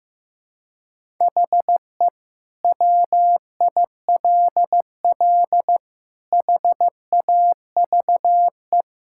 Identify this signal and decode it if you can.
Morse code, heard as a keyed tone, about 15 wpm, 700 Hz. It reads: HE WILL HAVE
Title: Sentences from Top 1000 Words - Review Part 6 - 15wpm